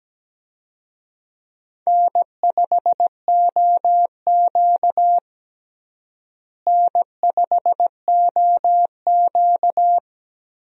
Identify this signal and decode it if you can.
Morse code, heard as a keyed tone, about 17 wpm, 700 Hz. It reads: N5OQ N5OQ
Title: USA Call Signs - 4 Characters - Repeated 2x - 17wpm